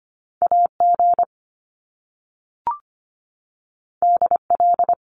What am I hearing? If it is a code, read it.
Morse code, heard as a keyed tone, about 25 wpm, 700 Hz. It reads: A G D L
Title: Top Two Letter Combinations in Words 25wpm